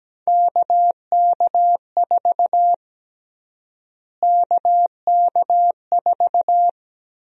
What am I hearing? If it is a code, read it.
Morse code, heard as a keyed tone, about 17 wpm, 700 Hz. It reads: KK4 KK4
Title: USA Call Sign Prefixes - Repeated 2x - 17wpm